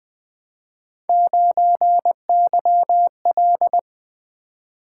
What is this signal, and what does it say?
Morse code, heard as a keyed tone, about 20 wpm, 700 Hz. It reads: 9YL
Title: Three Letter-Number Combinations - 20wpm